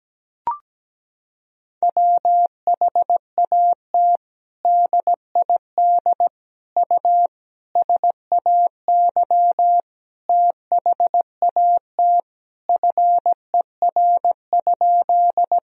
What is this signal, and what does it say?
Morse code, heard as a keyed tone, about 17 wpm, 700 Hz. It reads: WHAT DID U SAY THAT FER?
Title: Sentences from Top 200 Words and Abbreviations 17wpm